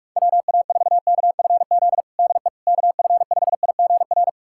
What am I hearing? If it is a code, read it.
Morse code, heard as a keyed tone, about 41 wpm, 700 Hz. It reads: WA4KFZ DE KF5IGR
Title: USA Callsign DE Callsign - 6 Characters - 45wpm